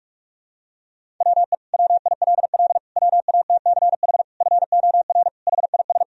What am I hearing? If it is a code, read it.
Morse code, heard as a keyed tone, about 46 wpm, 700 Hz. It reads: WE WILL WATCH FOR HIS
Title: Sentences from Top 1000 Words - Review Part 1 - 45wpm